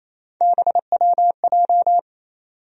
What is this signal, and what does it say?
Morse code, heard as a keyed tone, about 28 wpm, 700 Hz. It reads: BWJ
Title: USA Call Sign Postfixes - 3 Letters 28wpm